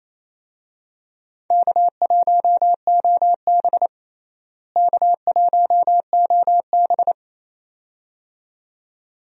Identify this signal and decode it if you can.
Morse code, heard as a keyed tone, about 28 wpm, 700 Hz. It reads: K1OB K1OB
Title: USA Call Signs - 4 Characters - Repeated 2x - 28wpm